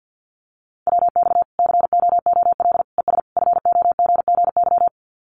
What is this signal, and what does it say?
Morse code, heard as a keyed tone, about 50 wpm, 700 Hz. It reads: WX COOL ES FOGGY
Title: Common QSO Phrases 50wpm